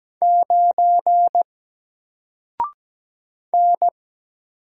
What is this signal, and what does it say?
Morse code, heard as a keyed tone, about 17 wpm, 700 Hz. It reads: 9 N
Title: Single Letter-Number Alt 17wpm